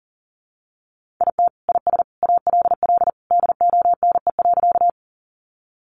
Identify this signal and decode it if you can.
Morse code, heard as a keyed tone, about 40 wpm, 700 Hz. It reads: IT IS ALL DONE.
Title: Sentences from Top 500 Words 40wpm